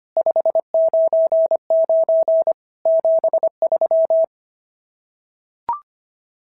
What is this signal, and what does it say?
Morse code, heard as a keyed tone, about 25 wpm, 590 Hz. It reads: 599 73
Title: SOTA QSO Practice 25wpm